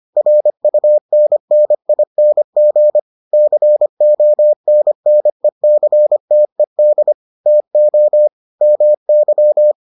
Morse code, heard as a keyed tone, about 25 wpm, 600 Hz. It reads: RUNNING CONNECTED TO MY